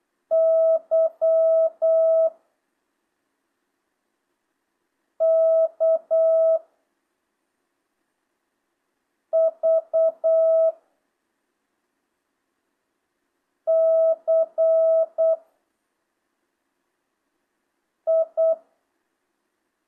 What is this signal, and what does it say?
Morse code, heard as a keyed tone, about 8 wpm, 650 Hz. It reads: Y K V C I
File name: jh8cbh / 2025cw